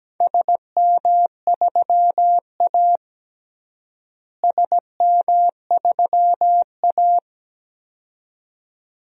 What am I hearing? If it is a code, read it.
Morse code, heard as a keyed tone, about 17 wpm, 700 Hz. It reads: SM3A SM3A
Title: DXCC-100 Call Signs - 4 Characters - Repeated 2x - 17wpm